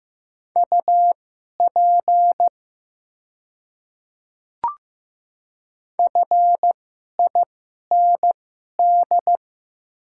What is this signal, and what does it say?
Morse code, heard as a keyed tone, about 15 wpm, spaced as for 11 wpm, 700 Hz. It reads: UP FIND
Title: 2x Character Spacing - Top 100 Words - 15wpm